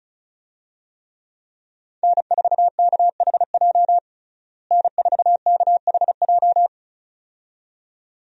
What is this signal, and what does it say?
Morse code, heard as a keyed tone, about 34 wpm, 700 Hz. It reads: N4KHJ N4KHJ
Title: USA Call Signs - 5 Characters - Repeated 2x - 35wpm